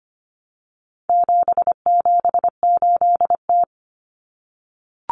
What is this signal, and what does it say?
Morse code, heard as a keyed tone, about 25 wpm, 700 Hz. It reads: 778T
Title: Common Cut Numbers - 4-Digit 25wpm